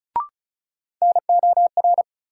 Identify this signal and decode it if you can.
Morse code, heard as a keyed tone, about 35 wpm, 700 Hz. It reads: NOR
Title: Sentences from Top 1500 Words - New Word Drill Alt - 35wpm